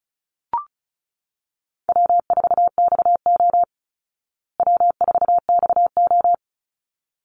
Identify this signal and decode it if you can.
Morse code, heard as a keyed tone, about 34 wpm, 700 Hz. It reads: W4XO W4XO
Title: Easy Call Signs Repeated 2x - 35wpm